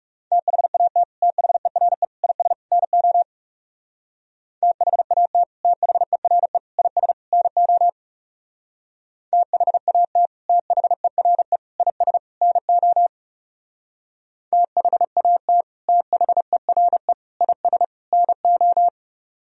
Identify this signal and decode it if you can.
Morse code, heard as a keyed tone, about 38 wpm, 700 Hz. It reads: THAT THERE IS NO THAT THERE IS NO THAT THERE IS NO THAT THERE IS NO